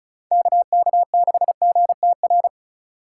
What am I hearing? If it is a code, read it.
Morse code, heard as a keyed tone, about 35 wpm, 700 Hz. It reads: KK6GTR
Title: Speed Racing - USA Call Signs 30wpm